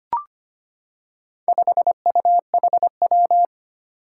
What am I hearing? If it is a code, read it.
Morse code, heard as a keyed tone, about 25 wpm, 700 Hz. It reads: 5UHW
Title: Four Challenging Letter-Number Combinations 25wpm